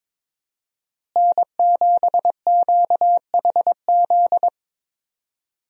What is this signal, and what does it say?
Morse code, heard as a keyed tone, about 22 wpm, 700 Hz. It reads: N7QHZ
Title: USA Call Signs - 5 Characters - Repeated 2x - 22wpm